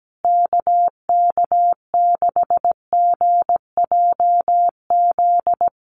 Morse code, heard as a keyed tone, about 17 wpm, 700 Hz. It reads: KK6GJZ
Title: USA Call Signs Repeated 3x 17wpm